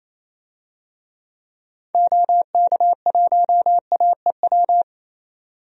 Morse code, heard as a keyed tone, about 28 wpm, 700 Hz. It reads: OK1AEW
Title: DXCC-100 Call Signs - 6 Characters - 28wpm